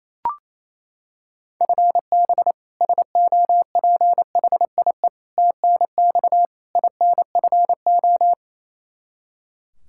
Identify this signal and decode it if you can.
Morse code, heard as a keyed tone, about 28 wpm, 700 Hz. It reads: FB SOPHIE TNX INFO
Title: K1USN SST QSO Practice 28wpm